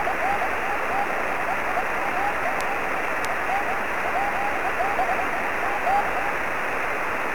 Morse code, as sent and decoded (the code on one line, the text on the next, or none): none